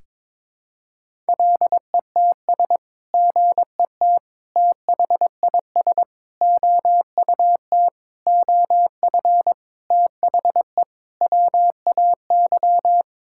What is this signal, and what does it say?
Morse code, heard as a keyed tone, about 22 wpm, 700 Hz. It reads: LETS GET THIS OUT OF THE WAY
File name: practice / 22wpm / Sets of 7 Words - 22wpm